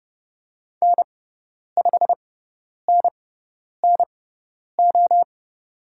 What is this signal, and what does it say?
Morse code, heard as a keyed tone, about 30 wpm, spaced as for 10 wpm, 700 Hz. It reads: N5NNO
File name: Lesson 18g - Letter m - Call Signs 30wpm